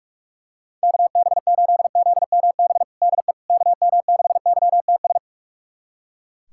Morse code, heard as a keyed tone, about 42 wpm, 700 Hz. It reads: KB8ZMB DE KM6YTS
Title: USA Callsign DE Callsign - 6 Characters - 45wpm